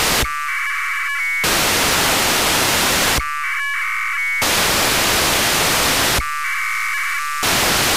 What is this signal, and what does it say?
Packet radio (AX.25, as in APRS).
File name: Horyu-4 10h28 UTC AFSK and CW JE9PEL